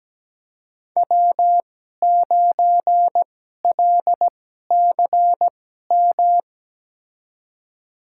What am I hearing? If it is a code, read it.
Morse code, heard as a keyed tone, about 17 wpm, spaced as for 12 wpm, 700 Hz. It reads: W9LCM
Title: USA Call Signs - 5 Characters - 2x Character Spacing - 17wpm